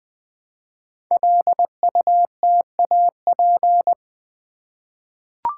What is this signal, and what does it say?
Morse code, heard as a keyed tone, about 20 wpm, 700 Hz. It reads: LUTAP
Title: Code Groups - 5 Letters - 20wpm